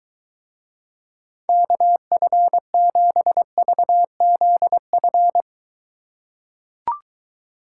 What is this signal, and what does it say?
Morse code, heard as a keyed tone, about 22 wpm, 700 Hz. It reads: KF7VZF